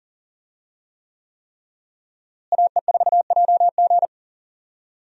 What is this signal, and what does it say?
Morse code, heard as a keyed tone, about 40 wpm, 700 Hz. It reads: AE4JG
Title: USA Call Signs - 5 Characters - Repeated 3x - 40wpm